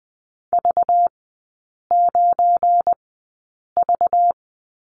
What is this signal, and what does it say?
Morse code, heard as a keyed tone, about 20 wpm, spaced as for 8 wpm, 690 Hz. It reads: V9V
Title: 191 Lesson 31e - Letter v - Three Characters 20wpm